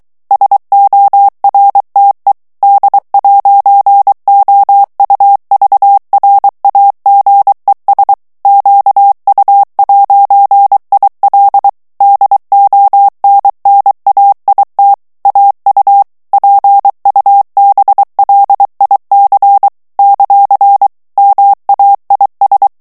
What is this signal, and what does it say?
Morse code, heard as a keyed tone, about 23 wpm, 800 Hz. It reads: SORTE D'OUVRAGES QU'IL DONNAIT AU PUBLIC ; MAIS
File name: morse mp3 / FRANCAIS TEXTE CLAIR 2 VITESSE 22 MOTS MINUTES